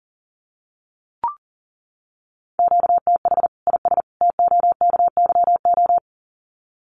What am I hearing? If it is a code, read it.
Morse code, heard as a keyed tone, about 40 wpm, 700 Hz. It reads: QTH IS TOKYO
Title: Common QSO Phrases 40wpm